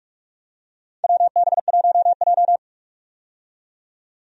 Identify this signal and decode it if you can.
Morse code, heard as a keyed tone, about 44 wpm, 700 Hz. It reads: WB1J
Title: USA Call Signs - 4 Characters - 45wpm